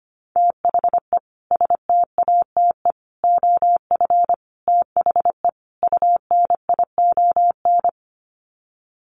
Morse code, heard as a keyed tone, about 25 wpm, 690 Hz. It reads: THE STATE OF THE UNION
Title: Sets of 5 Words Encore 25wpm